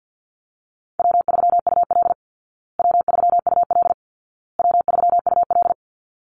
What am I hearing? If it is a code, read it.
Morse code, heard as a keyed tone, about 50 wpm, 700 Hz. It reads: W3UL W3UL W3UL
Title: Easy Call Signs Repeated 3x - 50wpm